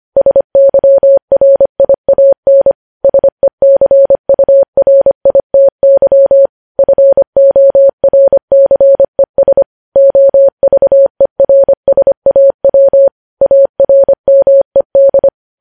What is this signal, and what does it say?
Morse code, heard as a keyed tone, about 25 wpm, 550 Hz. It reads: SYRIAN SECURITY FORCES OVERSAW ARMED
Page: News Headlines in Morse Code at 25 WPM